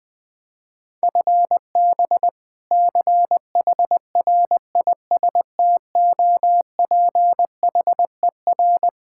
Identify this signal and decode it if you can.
Morse code, heard as a keyed tone, about 21 wpm, 700 Hz. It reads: FB CHRISTOPHER
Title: K1USN SST QSO Practice 20wpm